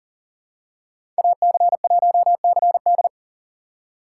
Morse code, heard as a keyed tone, about 40 wpm, 700 Hz. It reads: AC1CD
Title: USA Call Signs - 5 Characters - 6x Character Spacing - 40wpm